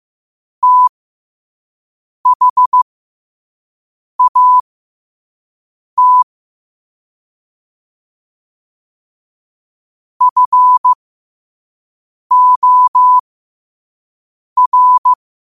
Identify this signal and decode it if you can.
Morse code, heard as a keyed tone, about 15 wpm, spaced as for 5 wpm, 1000 Hz. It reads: THAT FOR
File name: qotdfiles / 5wpm